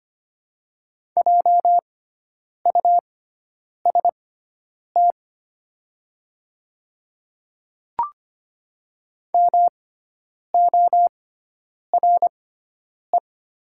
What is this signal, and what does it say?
Morse code, heard as a keyed tone, about 25 wpm, spaced as for 9 wpm, 700 Hz. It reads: JUST MORE
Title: 6x Character Spacing - Top 200 Words - 25wpm